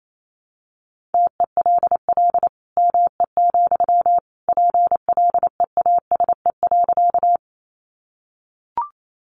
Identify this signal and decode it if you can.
Morse code, heard as a keyed tone, about 28 wpm, 700 Hz. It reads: TELL ME, PLEASE.
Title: Sentences from Top 1500 Words 28wpm